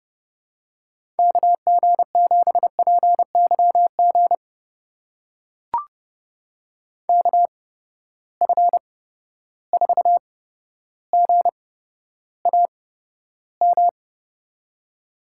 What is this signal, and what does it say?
Morse code, heard as a keyed tone, about 31 wpm, spaced as for 13 wpm, 700 Hz. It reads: KG7PYG KF4GAM
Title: USA Call Signs - 6 Characters - 8x Character Spacing - 30wpm